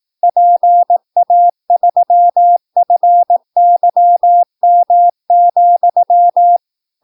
Morse code, heard as a keyed tone, about 18 wpm, 700 Hz. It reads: PA3FYM,
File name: PI4AA Uitzending